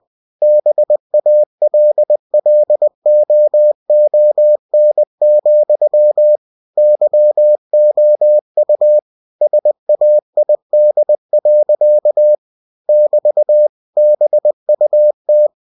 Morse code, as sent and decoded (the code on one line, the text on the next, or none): -... .- .-.. .-.. --- --- -. --..-- / -.-- --- ..- / ... .- .. -.. .-.-.- / -...- / -... ..- -
BALLOON, YOU SAID. <BT> BUT